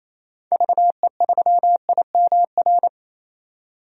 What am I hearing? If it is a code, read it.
Morse code, heard as a keyed tone, about 27 wpm, 700 Hz. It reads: VE3IMR
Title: DXCC-100 Call Signs - 6 Characters - 28wpm